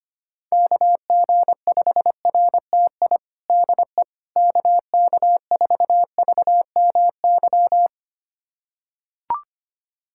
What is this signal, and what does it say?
Morse code, heard as a keyed tone, about 25 wpm, 700 Hz. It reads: KG5RTI DE KK4VMY
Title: USA Callsign DE Callsign 25wpm